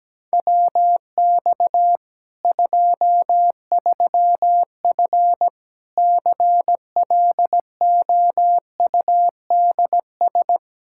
Morse code, heard as a keyed tone, about 17 wpm, 700 Hz. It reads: WX 23F CLOUDS
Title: Practice QSO - Exchange 2 - Info - 17wpm